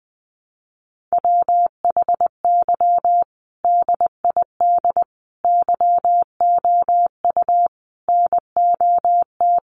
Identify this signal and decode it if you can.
Morse code, heard as a keyed tone, about 20 wpm, 700 Hz. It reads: WHY DID YOU NOT
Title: Sentences from Top 1200 Words 20wpm